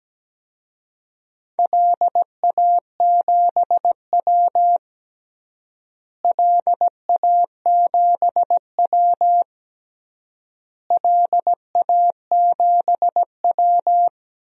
Morse code, heard as a keyed tone, about 17 wpm, 700 Hz. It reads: LA7W LA7W LA7W
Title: DXCC-100 Call Signs - 4 Characters - Repeated 3x - 17wpm